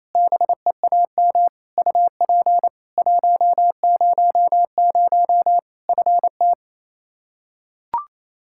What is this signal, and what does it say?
Morse code, heard as a keyed tone, about 28 wpm, 700 Hz. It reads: BEAM UP 100 FT
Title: Practice QSO - Exchange 2 - Info - 28wpm